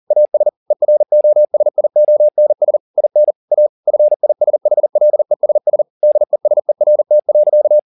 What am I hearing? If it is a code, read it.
Morse code, heard as a keyed tone, about 40 wpm, 600 Hz. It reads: AS EROSIONS IN A FISHLESS DESERT.